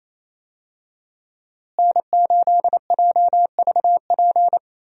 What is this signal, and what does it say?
Morse code, heard as a keyed tone, about 28 wpm, 700 Hz. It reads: N8JVP
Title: USA Call Signs - 5 Characters - Repeated 2x - 28wpm